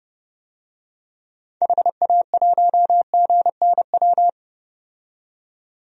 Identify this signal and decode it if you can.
Morse code, heard as a keyed tone, about 30 wpm, 700 Hz. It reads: HA1GNW
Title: DXCC-100 Call Signs - 6 Characters - 30wpm